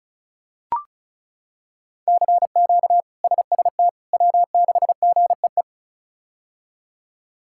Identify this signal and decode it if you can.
Morse code, heard as a keyed tone, about 34 wpm, 700 Hz. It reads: CQ SST W6GEE
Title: K1USN SST QSO Practice - 35wpm